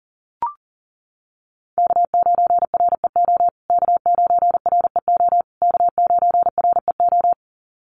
Morse code, heard as a keyed tone, about 40 wpm, 700 Hz. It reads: K9REO K9REO K9REO
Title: USA Call Signs Repeated 3x 40wpm